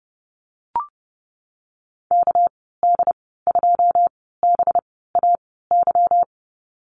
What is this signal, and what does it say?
Morse code, heard as a keyed tone, about 32 wpm, spaced as for 17 wpm, 700 Hz. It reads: KD2BAY
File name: Lesson 37g - Letter x - Call Signs 30wpm